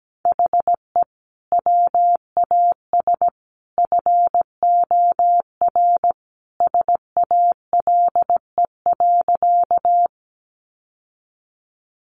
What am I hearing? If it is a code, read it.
Morse code, heard as a keyed tone, about 17 wpm, 700 Hz. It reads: HE WAS FOR SALE.
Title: Sentences from Top 1400 Words 17wpm